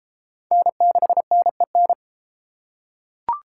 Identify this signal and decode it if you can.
Morse code, heard as a keyed tone, about 30 wpm, 700 Hz. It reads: N6NEN